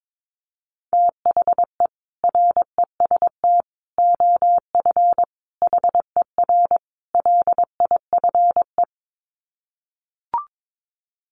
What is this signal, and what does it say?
Morse code, heard as a keyed tone, about 22 wpm, 700 Hz. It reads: THE REST OF HER LIFE
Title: Sets of 5 Words Encore 22wpm